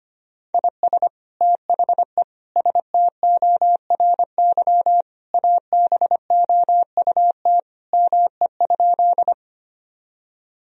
Morse code, heard as a keyed tone, about 25 wpm, 700 Hz. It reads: IS THE STORY ABOUT ME?
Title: Sentences from Top 1000 Words - Review Part 1 - 25wpm